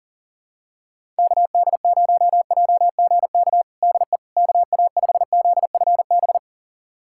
Morse code, heard as a keyed tone, about 40 wpm, 700 Hz. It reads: KD0JGK DE KA5ZFB
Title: USA Callsign DE Callsign - 6 Characters - 40wpm